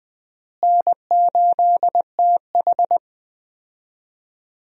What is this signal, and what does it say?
Morse code, heard as a keyed tone, about 21 wpm, 700 Hz. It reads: N8TH
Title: USA Call Signs - 4 Characters - Repeated 3x - 20wpm